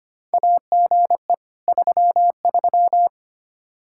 Morse code, heard as a keyed tone, about 25 wpm, 700 Hz. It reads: AGE 33
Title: Practice QSO - Exchange 3 - More Info - 25wpm